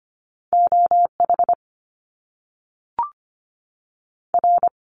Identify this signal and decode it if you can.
Morse code, heard as a keyed tone, about 25 wpm, 700 Hz. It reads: OH R
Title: Top Two Letter Combinations in Words 25wpm